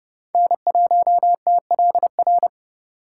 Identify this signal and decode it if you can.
Morse code, heard as a keyed tone, about 38 wpm, 700 Hz. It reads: N1TLR